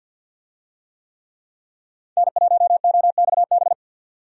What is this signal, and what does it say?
Morse code, heard as a keyed tone, about 50 wpm, 700 Hz. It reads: N1OXB
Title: USA Call Signs - 5 Characters - 50wpm